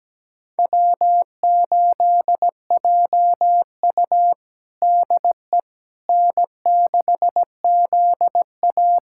Morse code, heard as a keyed tone, about 17 wpm, 700 Hz. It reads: W8JU DE N6ZA
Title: USA Callsign DE Callsign - 4 Characters - 17wpm